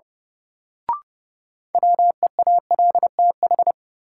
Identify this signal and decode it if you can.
Morse code, heard as a keyed tone, about 30 wpm, 700 Hz. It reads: WEALTH
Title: Sentences from Top 1000 Words - Review Part 4 - New Word Drill - 30wpm